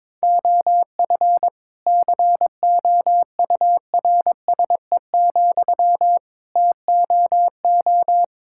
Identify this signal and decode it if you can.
Morse code, heard as a keyed tone, about 22 wpm, 700 Hz. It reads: OF COURSE, TOO
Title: Sentences from Top 1000 Words - Review Part 5 - 22wpm